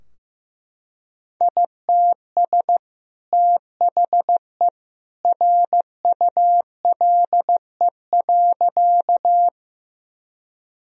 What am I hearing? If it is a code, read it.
Morse code, heard as a keyed tone, about 15 wpm, 700 Hz. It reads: ITS THE RULE.
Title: Sentences from Top 1000 Words - Review Part 2 - 15wpm